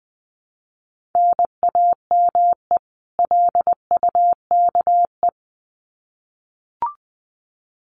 Morse code, heard as a keyed tone, about 20 wpm, 700 Hz. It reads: NAME LUKE
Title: Common QSO Phrases 20wpm